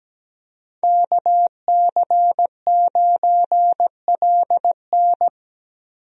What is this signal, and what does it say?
Morse code, heard as a keyed tone, about 17 wpm, 700 Hz. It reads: KC9LN
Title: USA Call Signs - 5 Characters - 17wpm